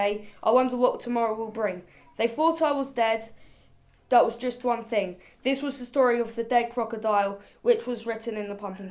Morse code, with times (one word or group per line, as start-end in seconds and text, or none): none